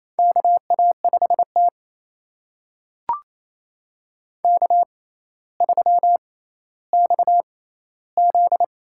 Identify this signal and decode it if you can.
Morse code, heard as a keyed tone, about 28 wpm, spaced as for 14 wpm, 700 Hz. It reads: KA5T K3XZ
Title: USA Call Signs - 4 Characters - 6x Character Spacing - 28wpm